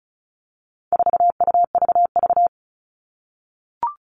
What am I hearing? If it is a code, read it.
Morse code, heard as a keyed tone, about 34 wpm, 700 Hz. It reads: 4UVV
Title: All Cut Numbers - 4-Digit 35wpm